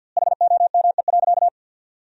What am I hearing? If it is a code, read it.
Morse code, heard as a keyed tone, about 51 wpm, 700 Hz. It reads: HOME.
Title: 2x Word-Spacing - Sentences from Top 500 Words - 50wpm